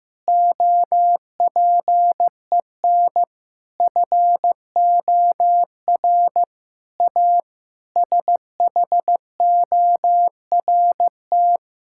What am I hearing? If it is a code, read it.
Morse code, heard as a keyed tone, about 16 wpm, 700 Hz. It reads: OPEN FOR A SHORT
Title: Sentences from Top 1600 Words 15wpm